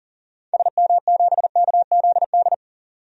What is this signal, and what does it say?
Morse code, heard as a keyed tone, about 37 wpm, 700 Hz. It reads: SM7KZD
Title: DXCC-100 Call Signs - 6 Characters - 40wpm